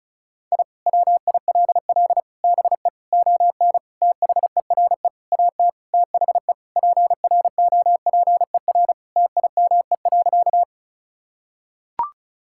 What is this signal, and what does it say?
Morse code, heard as a keyed tone, about 36 wpm, 700 Hz. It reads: I WILL BE ON THERE AT THE PROPER TIME.
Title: Sentences from Top 1000 Words - Review Part 5 - 35wpm